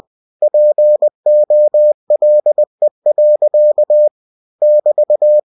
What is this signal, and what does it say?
Morse code, heard as a keyed tone, about 20 wpm, 600 Hz. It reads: POLE. <BT>